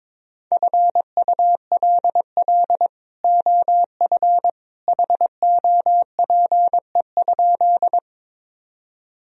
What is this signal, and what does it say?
Morse code, heard as a keyed tone, about 22 wpm, 700 Hz. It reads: FULL OF HOPE?
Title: Sentences from Top 1000 Words - Review Part 3 - 22wpm